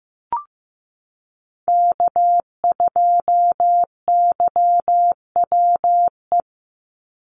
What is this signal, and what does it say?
Morse code, heard as a keyed tone, about 15 wpm, 700 Hz. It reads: K2YWE
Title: Call Signs Repeated 2x - 15wpm